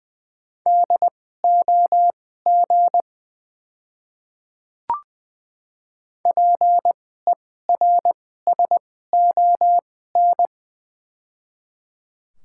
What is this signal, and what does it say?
Morse code, heard as a keyed tone, about 20 wpm, spaced as for 15 wpm, 690 Hz. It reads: DOG PERSON
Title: 2x Character Spacing - Top 500 Words - 20wpm